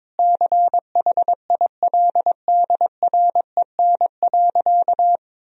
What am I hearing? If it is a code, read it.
Morse code, heard as a keyed tone, about 22 wpm, 700 Hz. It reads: CHILDREN.
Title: Sentences from Top 1000 Words - Review Part 3 - 22wpm